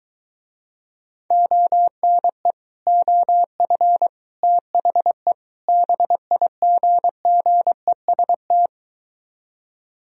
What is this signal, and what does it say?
Morse code, heard as a keyed tone, about 23 wpm, 700 Hz. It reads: ONE OF THE BIGGEST